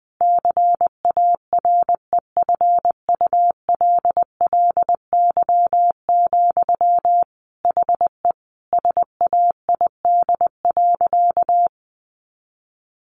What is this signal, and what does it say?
Morse code, heard as a keyed tone, about 20 wpm, 700 Hz. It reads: CAREFULLY, HE SAID.
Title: Sentences from Top 1300 Words 20wpm